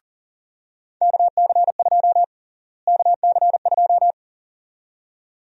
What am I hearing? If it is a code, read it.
Morse code, heard as a keyed tone, about 40 wpm, 700 Hz. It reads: KC2 KC2
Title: USA Call Sign Prefixes - Repeated 2x - 40wpm